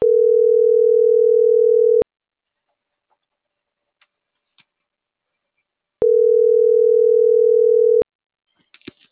Ringback tone (North America), 440 + 480 Hz.